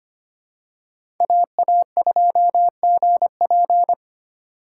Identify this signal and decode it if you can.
Morse code, heard as a keyed tone, about 25 wpm, 700 Hz. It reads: AA2GP
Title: USA Call Signs - 5 Characters - 3x Character Spacing - 25wpm